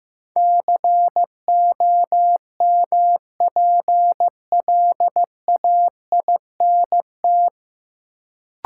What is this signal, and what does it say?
Morse code, heard as a keyed tone, about 15 wpm, 700 Hz. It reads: COMPLAINT
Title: Sentences from Top 1400 Words - New Word Drill - 15wpm